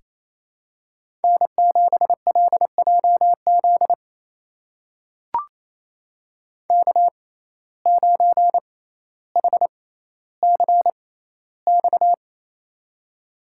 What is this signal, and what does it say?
Morse code, heard as a keyed tone, about 28 wpm, spaced as for 14 wpm, 700 Hz. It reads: N7LJZ K9HCX
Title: USA Call Signs - 5 Characters - 6x Character Spacing - 28wpm